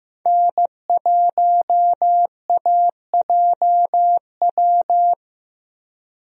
Morse code, heard as a keyed tone, about 15 wpm, 700 Hz. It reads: N1AJW
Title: USA Call Signs - 5 Characters - Repeated 2x - 15wpm